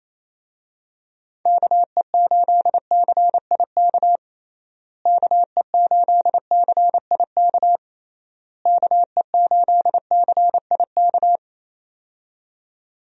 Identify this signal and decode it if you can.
Morse code, heard as a keyed tone, about 28 wpm, 700 Hz. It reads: KE8CIK KE8CIK KE8CIK
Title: USA Call Signs - 6 Characters - Repeated 3x - 28wpm